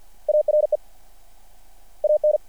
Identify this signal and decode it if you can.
Morse code, heard as a keyed tone, about 50 wpm, 600 Hz. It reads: ARE AN
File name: archived / CRSnet